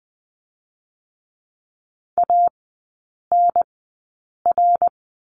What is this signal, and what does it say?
Morse code, heard as a keyed tone, about 20 wpm, spaced as for 8 wpm, 690 Hz. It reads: ANR
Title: 212 Lesson 34e - Number 8 - Three Characters 20wpm